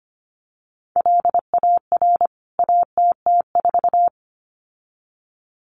Morse code, heard as a keyed tone, about 25 wpm, 700 Hz. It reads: LAR ATT4
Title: CWT Exchange 25wpm